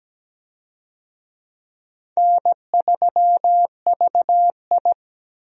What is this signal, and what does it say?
Morse code, heard as a keyed tone, about 17 wpm, 700 Hz. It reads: N3VI
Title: USA Call Signs - 4 Characters - 17wpm